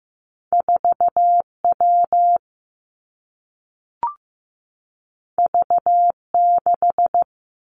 Morse code, heard as a keyed tone, about 15 wpm, 700 Hz. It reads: 4W V6
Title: Two Challenging Letter-Number Combinations 15wpm